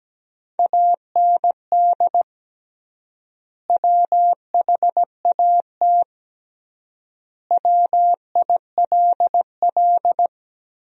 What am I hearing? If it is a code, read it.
Morse code, heard as a keyed tone, about 17 wpm, 700 Hz. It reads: AND WHAT WILL
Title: 3x Word-Spacing - Sentences from Top 200 Words - 17wpm